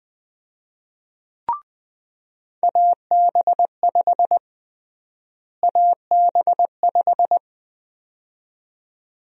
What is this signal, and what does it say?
Morse code, heard as a keyed tone, about 20 wpm, 700 Hz. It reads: AB5 AB5
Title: USA Call Sign Prefixes - Repeated 2x - 20wpm